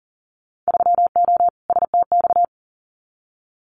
Morse code, heard as a keyed tone, about 40 wpm, 700 Hz. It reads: 3O STX
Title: Winter Field Day Exchanges 40wpm